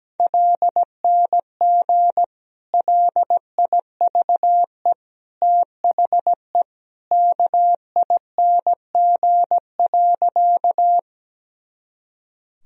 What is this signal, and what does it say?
Morse code, heard as a keyed tone, about 17 wpm, 700 Hz. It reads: LNG LIVE THE KING.
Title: Sentences from Top 400 Words and Abbreviations 17wpm